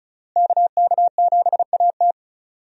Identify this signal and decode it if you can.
Morse code, heard as a keyed tone, about 36 wpm, 700 Hz. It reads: KK7AT
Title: USA Call Signs - 5 Characters - Repeated 3x - 35wpm